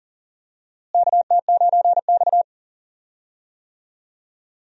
Morse code, heard as a keyed tone, about 40 wpm, 700 Hz. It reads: KT9X
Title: USA Call Signs - 4 Characters - Repeated 2x - 40wpm